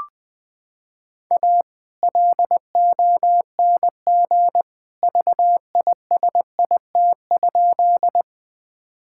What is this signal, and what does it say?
Morse code, heard as a keyed tone, about 21 wpm, 700 Hz. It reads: A LONG VISIT?
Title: Sentences from Top 1000 Words - Review Part 6 - 20wpm